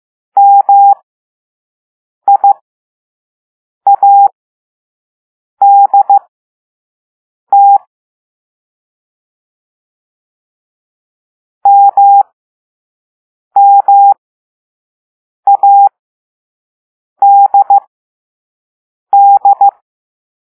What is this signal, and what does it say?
Morse code, heard as a keyed tone, about 15 wpm, spaced as for 5 wpm, 800 Hz. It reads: MIADT MMADD